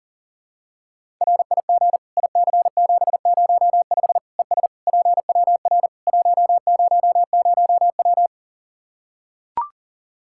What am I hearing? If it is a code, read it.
Morse code, heard as a keyed tone, about 40 wpm, 700 Hz. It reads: RIG IC705 ES PWR 100W
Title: Practice QSO - Exchange 2 - Info - 40wpm